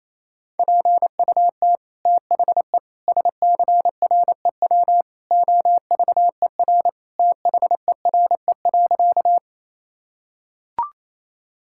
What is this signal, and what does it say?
Morse code, heard as a keyed tone, about 28 wpm, 700 Hz. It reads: PUT THE SCREW OVER THERE.
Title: Sentences from Top 1600 Words 28wpm